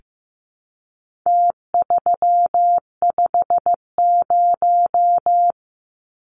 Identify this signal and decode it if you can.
Morse code, heard as a keyed tone, about 15 wpm, 700 Hz. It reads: T350
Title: Numbers - Contest Style - 4-Digit 15wpm